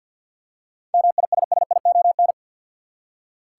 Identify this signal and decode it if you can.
Morse code, heard as a keyed tone, about 46 wpm, 700 Hz. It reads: MISSION
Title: Seven Letter Words - 50wpm